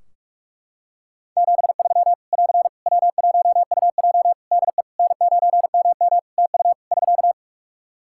Morse code, heard as a keyed tone, about 44 wpm, 700 Hz. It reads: 73 <AR> W1UJ DE N9MM TU <SK>
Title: Practice QSO - Exchange 4 - Ending - 45wpm